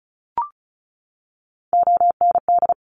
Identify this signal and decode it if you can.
Morse code, heard as a keyed tone, about 36 wpm, 700 Hz. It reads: OND
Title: Top Three Letter Combinations in Words 35wpm